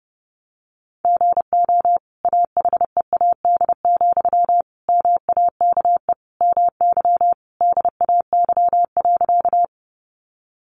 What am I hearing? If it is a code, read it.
Morse code, heard as a keyed tone, about 30 wpm, 700 Hz. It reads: GO AHEAD, MAKE MY DAY.
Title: Sentences from Top 1100 Words 30wpm